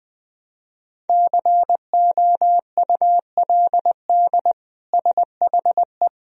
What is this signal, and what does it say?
Morse code, heard as a keyed tone, about 21 wpm, 700 Hz. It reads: COULD SHE
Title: Sentences from Top 1000 Words - Review Part 1 - 20wpm